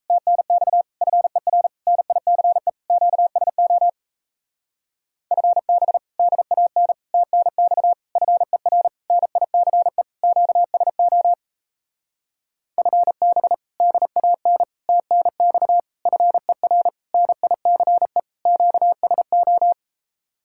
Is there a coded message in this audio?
Yes: Morse code from 0.1 to 19.7 s, heard as a keyed tone, about 37 wpm, 700 Hz. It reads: TNX FER NICE QSO FB DAN TNX FER NICE QSO FB DAN TNX FER NICE QSO